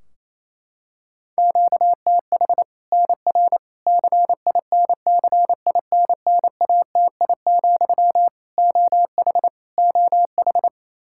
Morse code, heard as a keyed tone, about 28 wpm, 700 Hz. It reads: QTH NR CINCINNATI, OH OH
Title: Practice QSO - Exchange 1 - Beginning - 28wpm